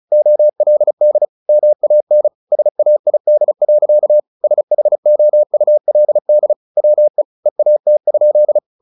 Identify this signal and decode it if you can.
Morse code, heard as a keyed tone, about 34 wpm, 600 Hz. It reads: OLD MAN SAID. SHOULD WE EAT?
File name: the old man and the sea / the old man and the sea 1 35wpm